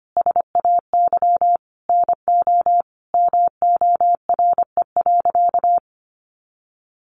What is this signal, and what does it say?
Morse code, heard as a keyed tone, about 25 wpm, 700 Hz. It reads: SAY NO MORE.
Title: Sentences from Top 1400 Words 25wpm